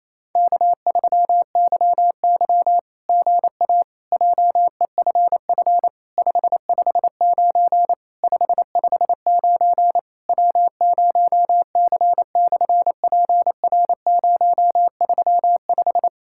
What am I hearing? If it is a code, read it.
Morse code, heard as a keyed tone, about 28 wpm, 690 Hz. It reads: K3YY GA JEFF 559 559 W0C/PR035
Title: SOTA QSO Practice 28wpm